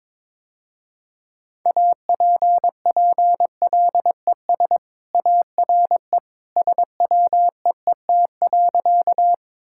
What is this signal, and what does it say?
Morse code, heard as a keyed tone, about 22 wpm, 700 Hz. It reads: APPLES ARE SWEET.
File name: practice / 22wpm / Easy Sentences with Hints - 22wpm